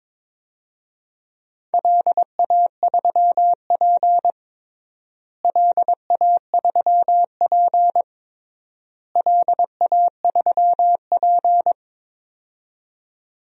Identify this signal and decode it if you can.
Morse code, heard as a keyed tone, about 22 wpm, 700 Hz. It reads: LA3P LA3P LA3P
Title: DXCC-100 Call Signs - 4 Characters - Repeated 3x - 22wpm